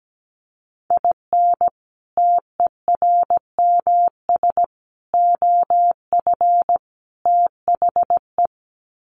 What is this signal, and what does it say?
Morse code, heard as a keyed tone, about 17 wpm, 700 Hz. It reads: IN TERMS OF THE
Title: Sets of 4 Words 17wpm